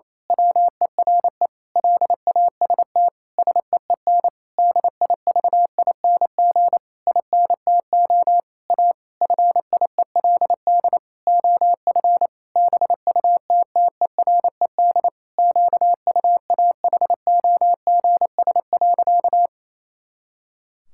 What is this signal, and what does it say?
Morse code, heard as a keyed tone, about 27 wpm, 700 Hz. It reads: WERE LAST SEEN DIVING INTO A FIELD OF BUTTERED QUAHOGS.
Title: Pangram Challenge v2 - 28wpm